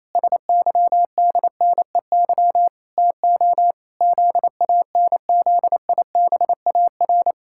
Morse code, heard as a keyed tone, about 28 wpm, 700 Hz. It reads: SYDNEY TO ZANZIBAR
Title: Pangram Challenge v2 - 28wpm